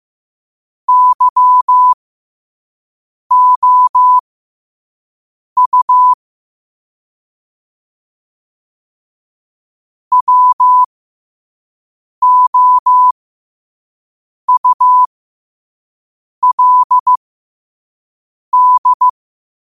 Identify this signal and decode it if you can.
Morse code, heard as a keyed tone, about 15 wpm, spaced as for 5 wpm, 1000 Hz. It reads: YOU WOULD
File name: 5wpm